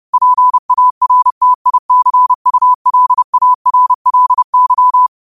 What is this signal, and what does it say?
Morse code, heard as a keyed tone, about 30 wpm, 1000 Hz. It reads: PARTICULARLY
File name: qotdfiles / 30wpm